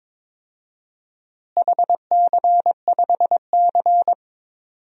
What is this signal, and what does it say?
Morse code, heard as a keyed tone, about 22 wpm, 700 Hz. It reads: HC5C
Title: DXCC-100 Call Signs - 4 Characters - Repeated 3x - 22wpm